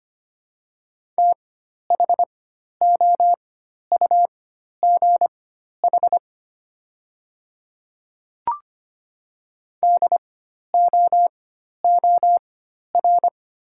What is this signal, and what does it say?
Morse code, heard as a keyed tone, about 25 wpm, spaced as for 12 wpm, 700 Hz. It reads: THOUGH DOOR
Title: 4x Character Spacing - Top 500 Words - 25wpm